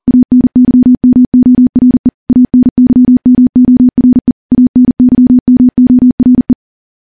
Morse code, heard as a keyed tone, about 43 wpm, 250 Hz. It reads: ANYMORE ANYMORE ANYMORE
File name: morse / morse-training-40WPM